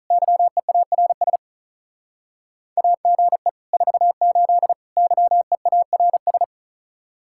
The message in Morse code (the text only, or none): YEARS AGE 48 YEARS